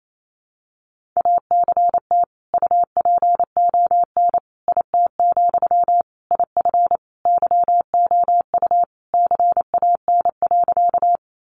Morse code, heard as a keyed tone, about 28 wpm, 700 Hz. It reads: ACT UPON IT, IF YOU CAN.
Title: Sentences from Top 1100 Words 28wpm